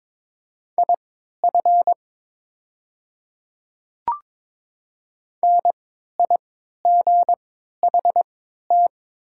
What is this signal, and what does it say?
Morse code, heard as a keyed tone, about 22 wpm, spaced as for 13 wpm, 700 Hz. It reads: IF NIGHT
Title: 3x Character Spacing - Top 300 Words - 22wpm